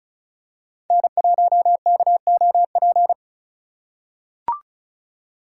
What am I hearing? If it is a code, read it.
Morse code, heard as a keyed tone, about 34 wpm, 700 Hz. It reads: N1KOP
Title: USA Call Signs - 5 Characters - 3x Character Spacing - 35wpm